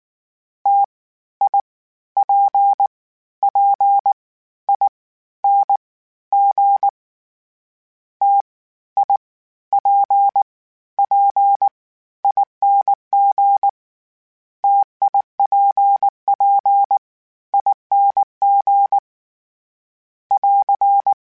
Morse code, heard as a keyed tone, about 19 wpm, 800 Hz. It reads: T I P P I N G T I P P ING TIPP ING <AR>
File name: Difficult Combos / Difficult Combos-mp3s / T AOEI @18